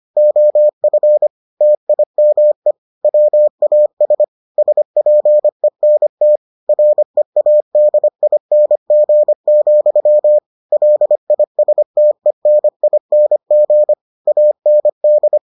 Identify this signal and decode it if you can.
Morse code, heard as a keyed tone, about 25 wpm, 600 Hz. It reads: OF TIME WAS SPENT READING, LISTENING AND